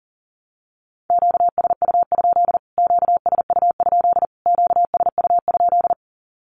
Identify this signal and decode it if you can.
Morse code, heard as a keyed tone, about 40 wpm, 700 Hz. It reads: QSU? QSU? QSU?
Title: Q-Codes Repeated 3x 40wpm